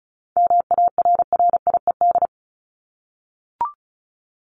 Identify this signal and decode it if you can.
Morse code, heard as a keyed tone, about 34 wpm, 700 Hz. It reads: MARRIED
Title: Top 1501-1750 Words 35wpm